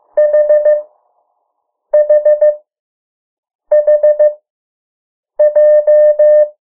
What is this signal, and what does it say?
Morse code, heard as a keyed tone, about 15 wpm, 600 Hz. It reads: H H H J
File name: morse mp3 / 03 urok 2